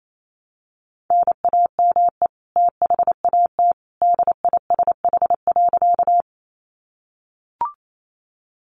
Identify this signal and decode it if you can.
Morse code, heard as a keyed tone, about 28 wpm, 700 Hz. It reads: NAME THAT DISH.
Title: Sentences from Top 1300 Words 28wpm